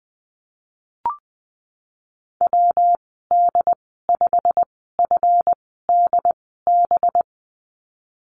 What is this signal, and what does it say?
Morse code, heard as a keyed tone, about 20 wpm, spaced as for 15 wpm, 700 Hz. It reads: WD5FDB
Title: 186 Lesson 30g - Letter b - Call Signs 20wpm